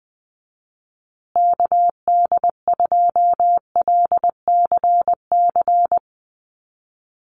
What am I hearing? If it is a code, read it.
Morse code, heard as a keyed tone, about 20 wpm, 700 Hz. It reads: KD2LCC
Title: USA Call Signs Repeated 2x 20wpm